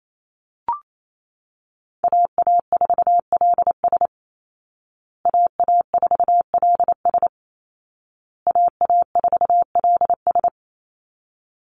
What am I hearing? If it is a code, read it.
Morse code, heard as a keyed tone, about 28 wpm, 700 Hz. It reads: AA4LS AA4LS AA4LS
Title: Call Signs Repeated 2x - 28wpm